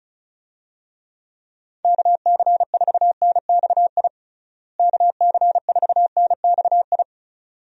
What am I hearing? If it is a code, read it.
Morse code, heard as a keyed tone, about 34 wpm, 700 Hz. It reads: KC4NXI KC4NXI
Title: USA Call Signs - 6 Characters - Repeated 2x - 35wpm